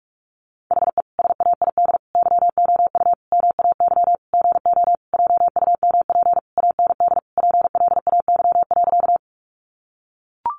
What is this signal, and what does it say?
Morse code, heard as a keyed tone, about 44 wpm, 700 Hz. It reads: HE SAID YOU MAY GO JUMP AND PLAY.
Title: Sentences from Top 1300 Words 45wpm